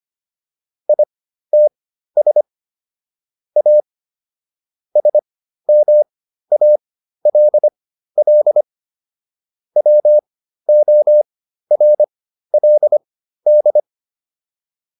Morse code, heard as a keyed tone, about 25 wpm, spaced as for 13 wpm, 600 Hz. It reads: ITS A SMALL WORLD
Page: intermediate-practice-files